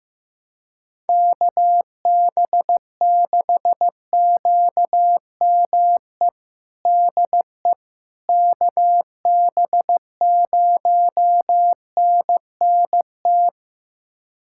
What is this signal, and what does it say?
Morse code, heard as a keyed tone, about 15 wpm, 700 Hz. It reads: KB6QME DE KB0NNT
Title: USA Callsign DE Callsign - 6 Characters - 15wpm